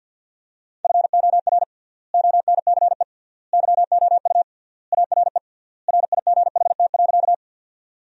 Morse code, heard as a keyed tone, about 51 wpm, 700 Hz. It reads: FOR ONCE YOU ARE RIGHT.
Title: 3x Word-Spacing - Sentences from Top 300 Words - 50wpm